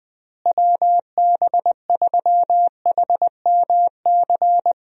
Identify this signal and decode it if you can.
Morse code, heard as a keyed tone, about 20 wpm, 700 Hz. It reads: WB3HMC
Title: USA Call Signs - 6 Characters - Repeated 2x - 20wpm